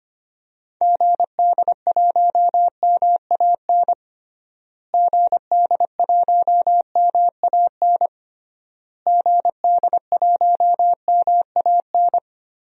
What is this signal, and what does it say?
Morse code, heard as a keyed tone, about 25 wpm, 700 Hz. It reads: GD1MAN GD1MAN GD1MAN
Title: DXCC-100 Call Signs - 6 Characters - Repeated 3x - 25wpm